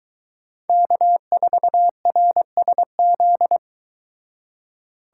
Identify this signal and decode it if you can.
Morse code, heard as a keyed tone, about 22 wpm, 700 Hz. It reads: K4RSZ